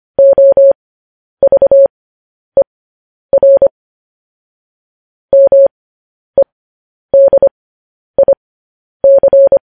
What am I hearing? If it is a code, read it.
Morse code, heard as a keyed tone, about 25 wpm, spaced as for 10 wpm, 550 Hz. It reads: OVER MEDIC
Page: news headlines in morse code at 10 wpm